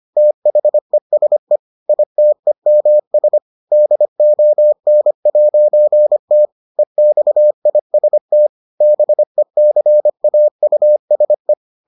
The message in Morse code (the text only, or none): THESE ITEMS DON'T EXIST BECAUSE